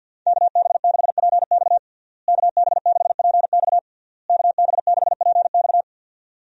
Morse code, heard as a keyed tone, about 47 wpm, 700 Hz. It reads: KB6PX KB6PX KB6PX
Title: USA Call Signs - 5 Characters - Repeated 3x - 50wpm